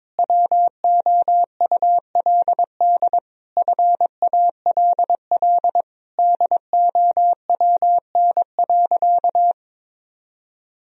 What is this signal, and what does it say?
Morse code, heard as a keyed tone, about 22 wpm, 700 Hz. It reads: WOULD FALL DOWN.
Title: Sentences from Top 1000 Words - Review Part 2 - 22wpm